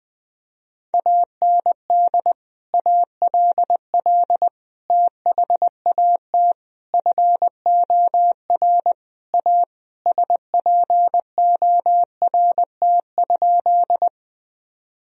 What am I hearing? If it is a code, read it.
Morse code, heard as a keyed tone, about 21 wpm, 700 Hz. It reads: AND ALL THAT FOR A SPORT?
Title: Sentences from Top 1000 Words - Review Part 4 - 20wpm